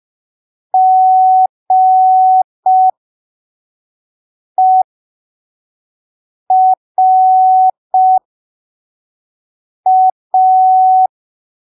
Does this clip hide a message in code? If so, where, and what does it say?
Yes: Morse code from 0.7 to 11.1 s, heard as a keyed tone, about 5 wpm, spaced as for 3 wpm, 750 Hz. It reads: GERA